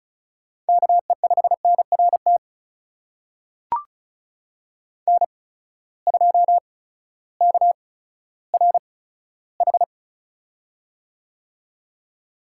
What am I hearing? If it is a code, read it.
Morse code, heard as a keyed tone, about 36 wpm, spaced as for 16 wpm, 700 Hz. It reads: KE5NRT N2KRH
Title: USA Call Signs - 8x Character Spacing - 35wpm